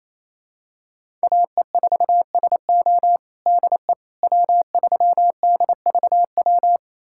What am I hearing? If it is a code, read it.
Morse code, heard as a keyed tone, about 28 wpm, 700 Hz. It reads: AE4SO DE W3DVW
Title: USA Callsign DE Callsign - 5 Characters - 28wpm